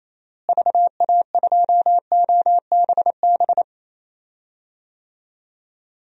Morse code, heard as a keyed tone, about 28 wpm, 700 Hz. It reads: VA2OBB
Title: DXCC-100 Call Signs - 6 Characters - Repeated 3x - 28wpm